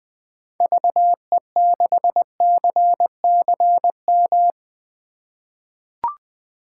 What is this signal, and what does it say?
Morse code, heard as a keyed tone, about 20 wpm, 700 Hz. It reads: VE6CCM
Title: 2023 SOTA Chaser Call Signs - 20wpm